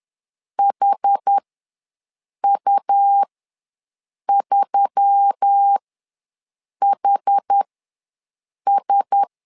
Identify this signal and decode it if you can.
Morse code, heard as a keyed tone, about 10 wpm, spaced as for 6 wpm, 800 Hz. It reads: HU3HS
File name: LEZIONI / 4 Q1S3 ws 10wpm 2437